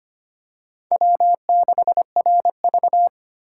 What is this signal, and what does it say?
Morse code, heard as a keyed tone, about 25 wpm, 700 Hz. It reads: W6RV
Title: USA Call Signs - 4 Characters - 2x Character Spacing - 25wpm